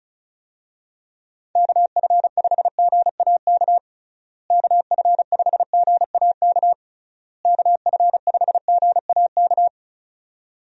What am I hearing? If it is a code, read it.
Morse code, heard as a keyed tone, about 34 wpm, 700 Hz. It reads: KF5GAK KF5GAK KF5GAK
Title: USA Call Signs - 6 Characters - Repeated 3x - 35wpm